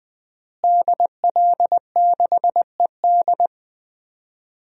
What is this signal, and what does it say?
Morse code, heard as a keyed tone, about 20 wpm, 700 Hz. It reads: DL6ED
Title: DXCC-100 Call Signs - 5 Characters - Repeated 2x - 20wpm